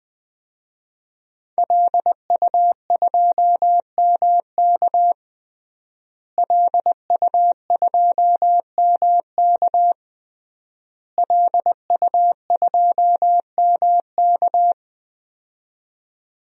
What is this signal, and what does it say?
Morse code, heard as a keyed tone, about 20 wpm, 700 Hz. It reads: LU2MK LU2MK LU2MK
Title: DXCC-100 Call Signs - 5 Characters - Repeated 3x - 20wpm